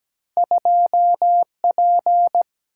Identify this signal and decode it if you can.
Morse code, heard as a keyed tone, about 17 wpm, 700 Hz. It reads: 2P
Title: Single-Letter Single-Number 17wpm